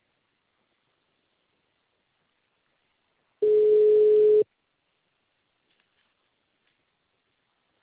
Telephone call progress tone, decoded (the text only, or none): ringback tone